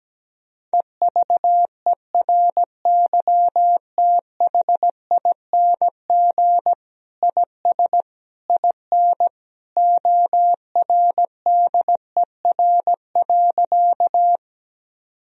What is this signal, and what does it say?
Morse code, heard as a keyed tone, about 17 wpm, 700 Hz. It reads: EVERYTHING IS IN ORDER.
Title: Sentences from Top 1000 Words - Review Part 2 - 17wpm